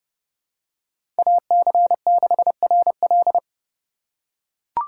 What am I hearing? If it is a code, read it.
Morse code, heard as a keyed tone, about 30 wpm, 700 Hz. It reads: AC6RL
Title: USA Call Signs - 5 Characters - 4x Character Spacing - 30wpm